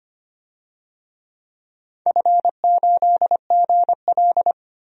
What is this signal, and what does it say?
Morse code, heard as a keyed tone, about 25 wpm, 700 Hz. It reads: F8GL
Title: DXCC-100 Call Signs - 4 Characters - 25wpm